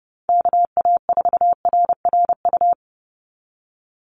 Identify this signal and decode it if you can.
Morse code, heard as a keyed tone, about 30 wpm, 700 Hz. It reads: KA4RRU
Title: Call Signs Repeated 3x - 30wpm